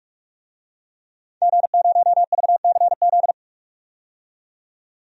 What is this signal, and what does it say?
Morse code, heard as a keyed tone, about 46 wpm, 700 Hz. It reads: G0VCZ
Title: DXCC-100 Call Signs - 5 Characters - 45wpm